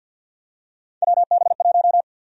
Morse code, heard as a keyed tone, about 47 wpm, 700 Hz. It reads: WB1
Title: USA Call Sign Prefixes - 50wpm